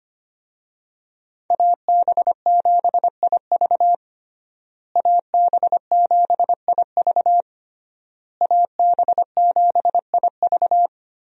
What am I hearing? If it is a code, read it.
Morse code, heard as a keyed tone, about 25 wpm, 700 Hz. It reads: AB7IV AB7IV AB7IV
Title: USA Call Signs - 5 Characters - Repeated 3x - 25wpm